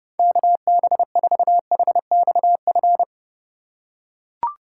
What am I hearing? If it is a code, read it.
Morse code, heard as a keyed tone, about 30 wpm, 700 Hz. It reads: KB4HXF